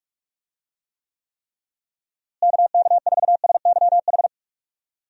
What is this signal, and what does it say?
Morse code, heard as a keyed tone, about 46 wpm, 700 Hz. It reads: KK4SYH
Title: USA Call Signs - 6 Characters - Repeated 3x - 45wpm